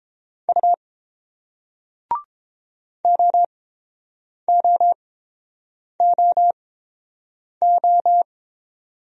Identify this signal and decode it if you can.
Morse code, heard as a keyed tone, about 30 wpm, spaced as for 7 wpm, 700 Hz. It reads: U OOOO